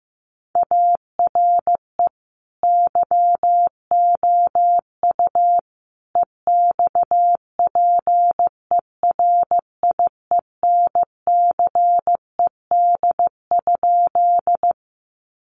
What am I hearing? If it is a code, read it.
Morse code, heard as a keyed tone, about 15 wpm, 700 Hz. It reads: ARE YOU EXPERIENCED?
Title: Sentences from Top 1100 Words 15wpm